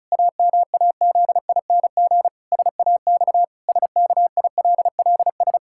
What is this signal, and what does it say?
Morse code, heard as a keyed tone, about 36 wpm, 700 Hz. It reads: AMAZING SAX SKILLS
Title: Pangram Challenge v2 - 35wpm